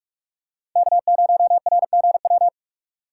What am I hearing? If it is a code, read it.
Morse code, heard as a keyed tone, about 47 wpm, 700 Hz. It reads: K0RGW
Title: USA Call Signs - 3x Character Spacing - 45wpm